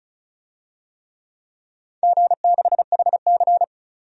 Morse code, heard as a keyed tone, about 36 wpm, 690 Hz. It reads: G6HC
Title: DXCC-100 Call Signs - 4 Characters - 35wpm